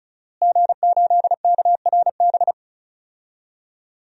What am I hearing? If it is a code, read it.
Morse code, heard as a keyed tone, about 34 wpm, 700 Hz. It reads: G8KRB
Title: DXCC-100 Call Signs - 5 Characters - Repeated 2x - 35wpm